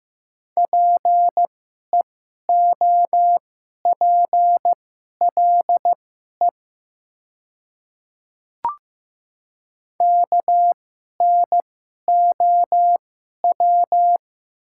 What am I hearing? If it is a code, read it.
Morse code, heard as a keyed tone, about 15 wpm, spaced as for 11 wpm, 700 Hz. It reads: PEOPLE KNOW
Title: 2x Character Spacing - Top 100 Words - 15wpm